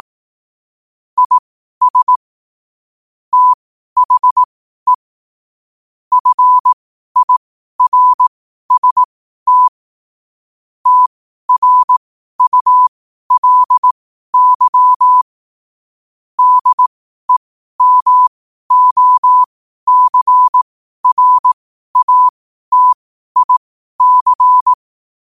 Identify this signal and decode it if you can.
Morse code, heard as a keyed tone, about 18 wpm, spaced as for 12 wpm, 1000 Hz. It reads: IS THE FIRST TRULY DEMOCRATIC
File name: qotdfiles / 12wpm